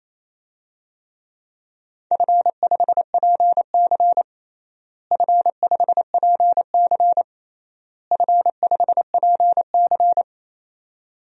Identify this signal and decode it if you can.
Morse code, heard as a keyed tone, about 28 wpm, 700 Hz. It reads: F5PC F5PC F5PC
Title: DXCC-100 Call Signs - 4 Characters - Repeated 3x - 28wpm